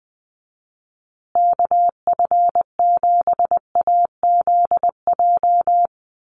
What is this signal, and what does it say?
Morse code, heard as a keyed tone, about 20 wpm, 700 Hz. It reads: KF7AZJ
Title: USA Call Signs Repeated 3x 20wpm